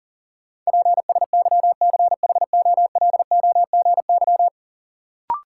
Morse code, heard as a keyed tone, about 40 wpm, 700 Hz. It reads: PSYCHOLOGY
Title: Sentences from Top 1400 Words - New Word Drill Alt - 40wpm